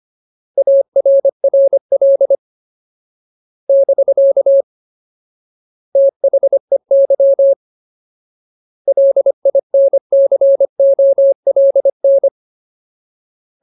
Morse code, heard as a keyed tone, about 25 wpm, 550 Hz. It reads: ARRL <BK> THEY LINCOLN
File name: Ultimate Word List / 25WPM Ultimate